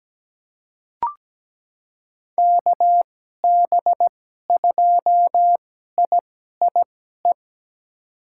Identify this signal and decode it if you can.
Morse code, heard as a keyed tone, about 17 wpm, spaced as for 12 wpm, 700 Hz. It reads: KB2IIE
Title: USA Call Signs - 6 Characters - 2x Character Spacing - 17wpm